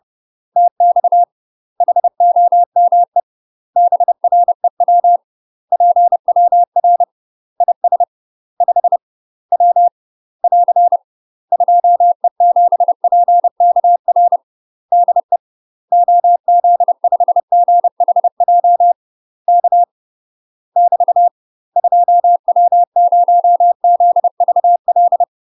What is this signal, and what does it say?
Morse code, heard as a keyed tone, about 30 wpm, 700 Hz. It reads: TX HOME BREW PWR IS 5 W <AR> 2E7PKR DE OZ5GHJ K <BT> 2W0ZVL
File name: UK MASTER QSOs TRAINING / 30WPM / UK MASTER QSOs 0048